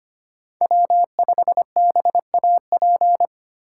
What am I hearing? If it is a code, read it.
Morse code, heard as a keyed tone, about 25 wpm, 700 Hz. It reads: W5BAP
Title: USA Call Signs - 5 Characters - 2x Character Spacing - 25wpm